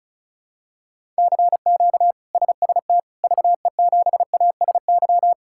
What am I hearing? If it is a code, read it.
Morse code, heard as a keyed tone, about 34 wpm, 700 Hz. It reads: CQ SST VE7ASY
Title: K1USN SST QSO Practice - 35wpm